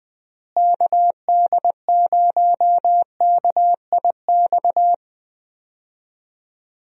Morse code, heard as a keyed tone, about 20 wpm, 700 Hz. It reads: KD0KIX
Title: USA Call Signs - 6 Characters - Repeated 3x - 20wpm